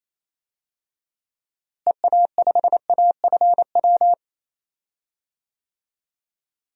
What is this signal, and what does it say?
Morse code, heard as a keyed tone, about 28 wpm, 700 Hz. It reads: EA5AFW
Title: DXCC-100 Call Signs - 6 Characters - Repeated 2x - 28wpm